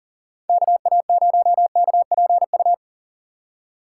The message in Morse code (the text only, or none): KA0KPU